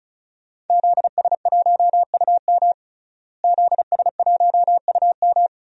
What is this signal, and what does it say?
Morse code, heard as a keyed tone, about 35 wpm, 700 Hz. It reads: ZS1UM ZS1UM
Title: DXCC-100 Call Signs - 5 Characters - Repeated 3x - 35wpm